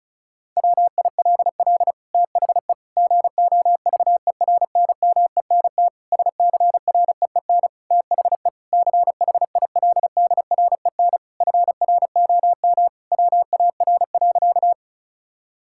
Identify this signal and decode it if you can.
Morse code, heard as a keyed tone, about 36 wpm, 700 Hz. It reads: WILL THE GOVERNMENT SCREEN THE CHILDREN FROM WAR.
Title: Sentences from Top 1000 Words - Review Part 6 - 35wpm